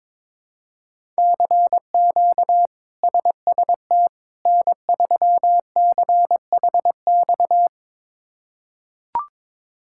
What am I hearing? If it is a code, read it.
Morse code, heard as a keyed tone, about 22 wpm, 700 Hz. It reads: CQ SST N3CHX
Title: K1USN SST QSO Practice - 22wpm